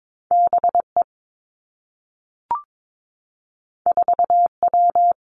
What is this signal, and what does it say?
Morse code, heard as a keyed tone, about 22 wpm, 700 Hz. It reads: B E 4 W
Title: Two Challenging Letter-Number Combinations 22wpm